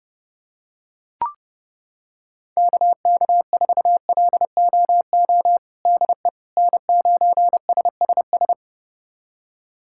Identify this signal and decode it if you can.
Morse code, heard as a keyed tone, about 30 wpm, 700 Hz. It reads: KK4LOO DE N9SSS
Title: USA Callsign DE Callsign 30wpm